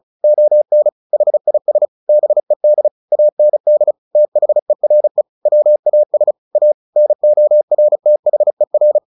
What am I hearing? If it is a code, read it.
Morse code, heard as a keyed tone, about 34 wpm, 600 Hz. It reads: ON HIS BED AND THERE WAS A NORTHER
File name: the old man and the sea 2 35wpm